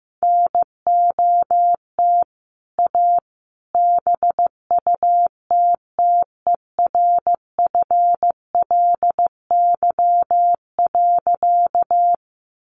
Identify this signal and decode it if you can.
Morse code, heard as a keyed tone, about 15 wpm, 700 Hz. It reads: NOT A BUTTERFLY.
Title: A Sound Of Thunder 15wpm